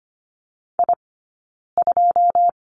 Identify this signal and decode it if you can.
Morse code, heard as a keyed tone, about 25 wpm, 700 Hz. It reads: I 2
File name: Lesson 15d - Number 5 - Two Characters 25wpm